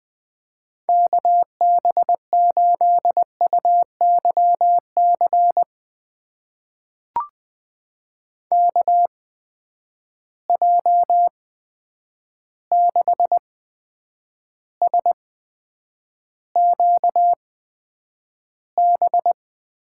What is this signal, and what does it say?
Morse code, heard as a keyed tone, about 20 wpm, spaced as for 9 wpm, 700 Hz. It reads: KB8UYC KJ6SQB
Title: USA Call Signs - 6 Characters - 8x Character Spacing - 20wpm